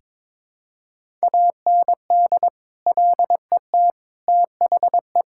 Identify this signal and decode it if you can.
Morse code, heard as a keyed tone, about 22 wpm, 700 Hz. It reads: AND LET THE
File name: practice / 22wpm / Sets of 8 Words - 22wpm